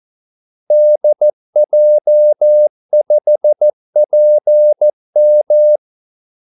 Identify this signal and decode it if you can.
Morse code, heard as a keyed tone, about 14 wpm, 600 Hz. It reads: DJ5PM
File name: Callsigns-14wpm